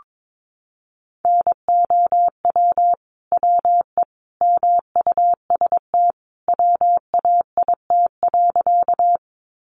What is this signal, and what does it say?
Morse code, heard as a keyed tone, about 22 wpm, 700 Hz. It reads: NOW WE MUST WAIT.
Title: Sentences from Top 500 Words 22wpm